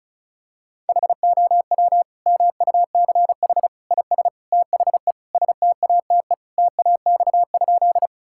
Morse code, heard as a keyed tone, about 34 wpm, 700 Hz. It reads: HOW MUCH IS THE STATE TAX?
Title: Sentences from Top 1000 Words - Review Part 5 - 35wpm